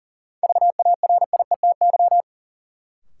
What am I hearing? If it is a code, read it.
Morse code, heard as a keyed tone, about 40 wpm, 700 Hz. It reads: VARIETY
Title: Sentences from Top 1000 Words - Review Part 4 - New Word Drill - 40wpm